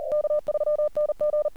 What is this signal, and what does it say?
Morse code, heard as a keyed tone, about 40 wpm, 600 Hz. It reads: K3NG
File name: archived / CRSnet